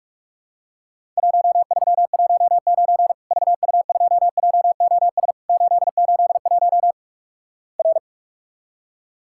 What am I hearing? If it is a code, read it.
Morse code, heard as a keyed tone, about 46 wpm, 700 Hz. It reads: 1319 VU2JOS 881 R
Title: Worked All Europe DX Contest - QTC Exchange - 45wpm